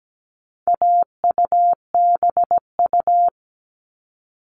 Morse code, heard as a keyed tone, about 17 wpm, 700 Hz. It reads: AUBU
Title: All Cut Numbers - 4-Digit 17wpm